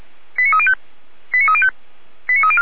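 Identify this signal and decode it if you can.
Five-tone selective call (ZVEI), heard as quick runs of five tones, tones of about 70 ms. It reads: 89305 89305 89304
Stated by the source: ZVEI five-tone selective call received by SDR, from the audio server of Feuerwehr Orlamünde (fire brigade).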